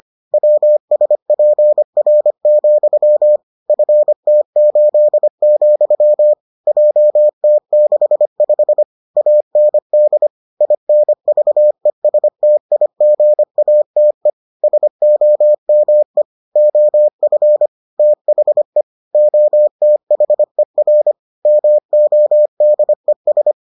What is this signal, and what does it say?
Morse code, heard as a keyed tone, about 25 wpm, 600 Hz. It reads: WSPR, FT8, JT65 AND INVESTIGATE SOME OF THE OTHER MODES